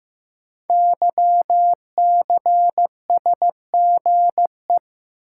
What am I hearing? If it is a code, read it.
Morse code, heard as a keyed tone, about 15 wpm, 700 Hz. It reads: YCSGE
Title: Code Groups - 5 Letters - 15wpm